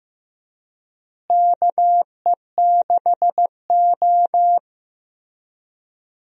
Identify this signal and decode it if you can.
Morse code, heard as a keyed tone, about 15 wpm, 700 Hz. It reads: KE6O
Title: USA Call Signs - 4 Characters - Repeated 3x - 15wpm